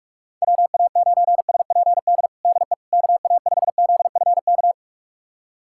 Morse code, heard as a keyed tone, about 46 wpm, 700 Hz. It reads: WA9SPD DE KA5ZFK
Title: USA Callsign DE Callsign - 6 Characters - 45wpm